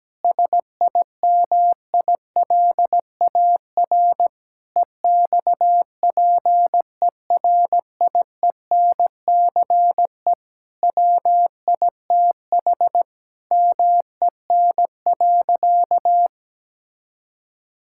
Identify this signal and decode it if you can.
Morse code, heard as a keyed tone, about 17 wpm, 700 Hz. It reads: SIMILAR EXPERIENCE WITH MEN.
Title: Sentences from Top 1000 Words - Review Part 5 - 17wpm